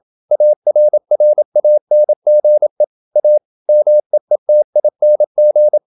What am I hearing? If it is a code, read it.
Morse code, heard as a keyed tone, about 27 wpm, 600 Hz. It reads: ARRANGE A MEETING
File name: Never 0027wpm